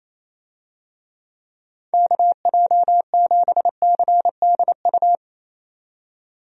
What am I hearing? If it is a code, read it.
Morse code, heard as a keyed tone, about 28 wpm, 700 Hz. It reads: KJ7CDU
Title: USA Call Signs - 6 Characters - 28wpm